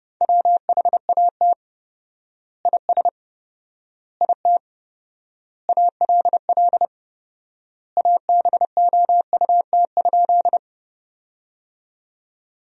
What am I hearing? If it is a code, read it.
Morse code, heard as a keyed tone, about 30 wpm, 700 Hz. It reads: WHAT IS IT ALL ABOUT?
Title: 4x Word-Spacing - Sentences from Top 100 Words - 30wpm